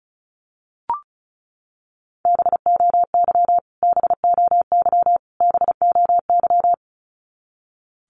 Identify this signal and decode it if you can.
Morse code, heard as a keyed tone, about 34 wpm, 700 Hz. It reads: BOY BOY BOY
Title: Top 300 Words Repeated 35wpm